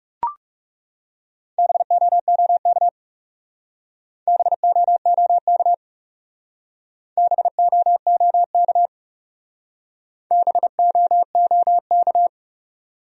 Morse code, heard as a keyed tone, about 40 wpm, 700 Hz. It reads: BOOK BOOK BOOK BOOK